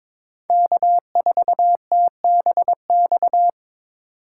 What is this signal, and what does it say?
Morse code, heard as a keyed tone, about 22 wpm, 690 Hz. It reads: K4TBX
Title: USA Call Signs - 5 Characters - 2x Character Spacing - 22wpm